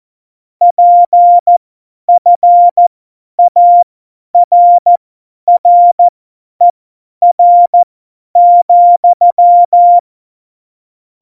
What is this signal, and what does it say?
Morse code, heard as a keyed tone, about 14 wpm, spaced as for 10 wpm, 700 Hz. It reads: PFARRER,
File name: Morsekurs / Witze / Witz 08. Dezember 2020, 50 F70